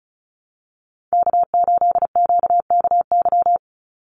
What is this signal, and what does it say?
Morse code, heard as a keyed tone, about 34 wpm, 700 Hz. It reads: K8QKY
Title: Call Signs - 35wpm